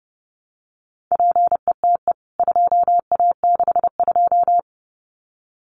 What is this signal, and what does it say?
Morse code, heard as a keyed tone, about 30 wpm, 700 Hz. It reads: PETE 2A62
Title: CWT Exchange 30wpm